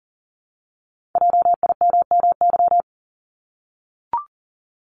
Morse code, heard as a keyed tone, about 42 wpm, 700 Hz. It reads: JIMMY
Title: Multi-letter Phonemes Repeated 3x 40wpm